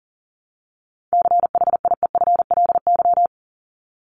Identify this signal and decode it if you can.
Morse code, heard as a keyed tone, about 40 wpm, 700 Hz. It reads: CHIEFLY
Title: LY Suffix - Top 100 Words with Base Words 40wpm